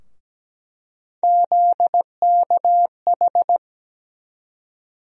Morse code, heard as a keyed tone, about 17 wpm, 700 Hz. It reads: ZKH
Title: USA Call Sign Postfixes - 3 Letters 17wpm